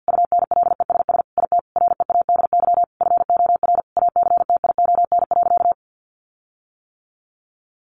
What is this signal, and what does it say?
Morse code, heard as a keyed tone, about 50 wpm, 700 Hz. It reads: UNLESS IT READY FOR ACTION.
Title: Sentences from Top 1300 Words 50wpm